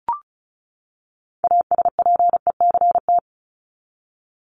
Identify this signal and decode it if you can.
Morse code, heard as a keyed tone, about 34 wpm, 700 Hz. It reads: ASPECT
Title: Top 1251-1500 Words 35wpm